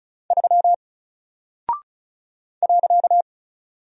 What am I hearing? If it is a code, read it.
Morse code, heard as a keyed tone, about 35 wpm, 700 Hz. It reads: 3 .
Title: Single Character (Letter:Number:Punctuation:Prosign) 35wpm